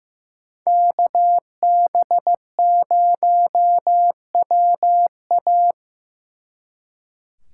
Morse code, heard as a keyed tone, about 18 wpm, 700 Hz. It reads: KB0WA
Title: Speed Racing - USA 2x2 Call Signs 15wpm